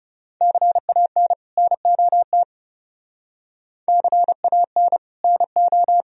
Morse code, heard as a keyed tone, about 34 wpm, 700 Hz. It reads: CAN NOT CAN NO
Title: Speed-Racing - Sets of 2 Words 30wpm